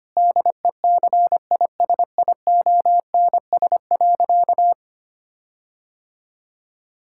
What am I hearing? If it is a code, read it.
Morse code, heard as a keyed tone, about 25 wpm, 700 Hz. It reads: DECISIONS.
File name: practice / 25wpm / Difficult Sentences with Hints - 25wpm